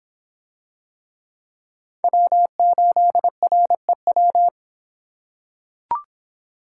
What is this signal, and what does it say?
Morse code, heard as a keyed tone, about 25 wpm, 700 Hz. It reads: W8REW